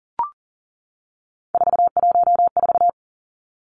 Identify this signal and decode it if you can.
Morse code, heard as a keyed tone, about 40 wpm, 700 Hz. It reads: 414
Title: Numbers - 3-Digit 40wpm